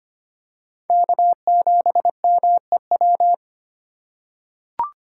Morse code, heard as a keyed tone, about 25 wpm, 700 Hz. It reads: K7MEW
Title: USA Call Signs - 5 Characters - 2x Character Spacing - 25wpm